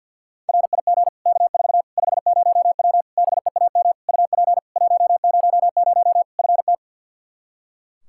Morse code, heard as a keyed tone, about 47 wpm, 700 Hz. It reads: RIG K4 50W BEAM UP 100 FT
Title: Practice QSO - Exchange 2 - Info - 50wpm